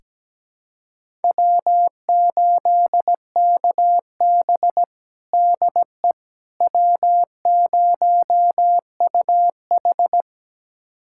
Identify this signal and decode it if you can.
Morse code, heard as a keyed tone, about 17 wpm, 700 Hz. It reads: W8KB DE W0UH
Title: USA Callsign DE Callsign - 4 Characters - 17wpm